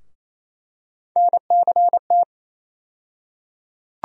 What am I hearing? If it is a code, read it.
Morse code, heard as a keyed tone, about 28 wpm, 700 Hz. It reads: NCT
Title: USA Call Sign Postfixes - 3 Letters 28wpm